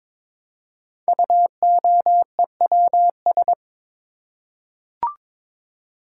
Morse code, heard as a keyed tone, about 22 wpm, 700 Hz. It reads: UOEWS
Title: Code Groups - 5 Letters - 22wpm